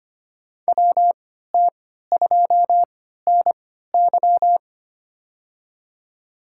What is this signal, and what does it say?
Morse code, heard as a keyed tone, about 25 wpm, spaced as for 14 wpm, 700 Hz. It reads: WT2NY
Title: USA Call Signs - 3x Character Spacing - 25wpm